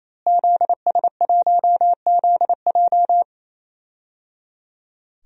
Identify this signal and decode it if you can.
Morse code, heard as a keyed tone, about 28 wpm, 700 Hz. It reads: ZS1ZJ
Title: DXCC-100 Call Signs - 5 Characters - Repeated 3x - 28wpm